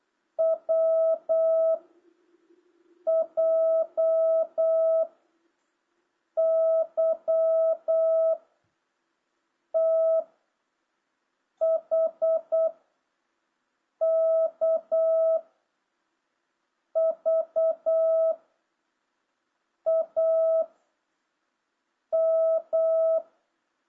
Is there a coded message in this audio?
Yes: Morse code from 0.4 to 23.2 s, heard as a keyed tone, about 8 wpm, 650 Hz. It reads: W J Y T H K V A M